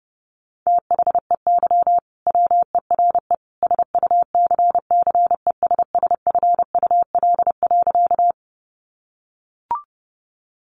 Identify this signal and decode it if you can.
Morse code, heard as a keyed tone, about 30 wpm, 700 Hz. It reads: THEY WERE SUCCESSFUL.
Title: Sentences from Top 900 Words 30wpm